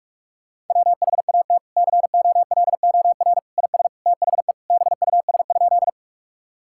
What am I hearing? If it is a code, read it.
Morse code, heard as a keyed tone, about 46 wpm, 700 Hz. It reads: WHAT COLOR IS THE BUS?
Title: Sentences from Top 1000 Words - Review Part 6 - 45wpm